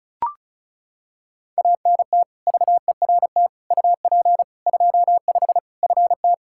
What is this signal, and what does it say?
Morse code, heard as a keyed tone, about 36 wpm, 700 Hz. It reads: ANT VERT UP 25 FT
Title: Practice QSO - Exchange 2 - Info - 35wpm